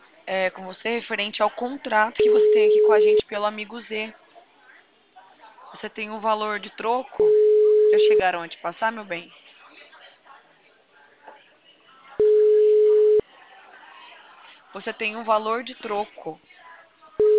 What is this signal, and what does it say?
Ringback tone (Europe, 425 Hz), with speech.
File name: out- -2030- - - .79